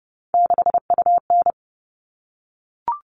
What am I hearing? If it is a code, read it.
Morse code, heard as a keyed tone, about 30 wpm, 700 Hz. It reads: 6UN
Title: All Cut Numbers - 3-Digit 30wpm